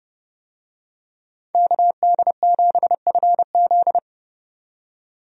Morse code, heard as a keyed tone, about 30 wpm, 700 Hz. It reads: KD7FZ
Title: USA Call Signs - 5 Characters - 30wpm